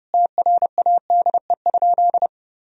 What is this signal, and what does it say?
Morse code, heard as a keyed tone, about 30 wpm, 700 Hz. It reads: TRADE?
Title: Sentences from Top 1000 Words - Review Part 4 - 30wpm